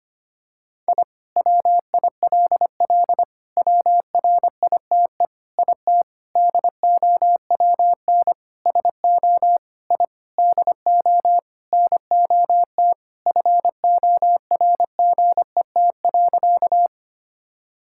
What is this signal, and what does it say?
Morse code, heard as a keyed tone, about 25 wpm, 700 Hz. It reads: I WILL WRITE IT DOWN SO I DO NOT FORGET.
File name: practice / 25wpm / Sentences with 4-Word Phrases - 25wpm